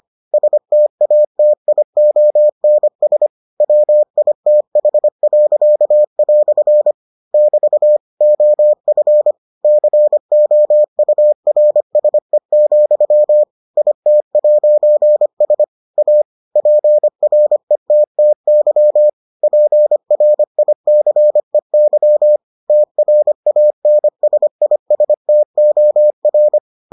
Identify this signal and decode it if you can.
Morse code, heard as a keyed tone, about 25 wpm, 600 Hz. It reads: STATIONS WITH." <BT> OF COURSE, IT'S A PRETTY PRICEY TRANSISTOR